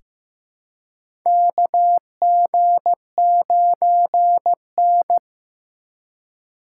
Morse code, heard as a keyed tone, about 15 wpm, 700 Hz. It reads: KG9N
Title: USA Call Signs - 3x Character Spacing - 15wpm